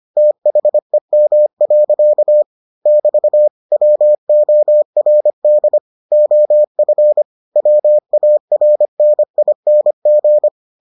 Morse code, heard as a keyed tone, about 25 wpm, 600 Hz. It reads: THEM. <BT> WORD OF WARNING